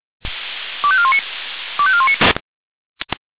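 Five-tone selective call (ZVEI), heard as quick runs of five tones, tones of about 70 ms. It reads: 25419 25419
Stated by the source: ZVEI five-tone selective call received on an SDR.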